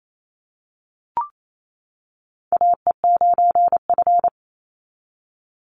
Morse code, heard as a keyed tone, about 28 wpm, 700 Hz. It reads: AE9F
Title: NA SOTA Chaser Callsigns 28wpm